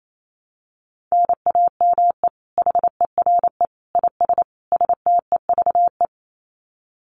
Morse code, heard as a keyed tone, about 28 wpm, 700 Hz. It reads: NAME HERE IS STEVE
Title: Common QSO Phrases 28wpm